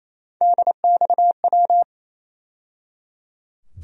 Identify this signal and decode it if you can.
Morse code, heard as a keyed tone, about 28 wpm, 700 Hz. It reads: DXW
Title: USA Call Sign Postfixes - 3 Letters 28wpm